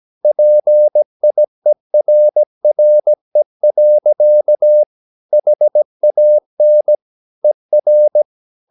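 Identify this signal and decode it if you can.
Morse code, heard as a keyed tone, about 17 wpm, 600 Hz. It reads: PIERRE. HAN ER
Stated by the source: Slow (QRS) Morse practice at 17 wpm, from CW med Gnister.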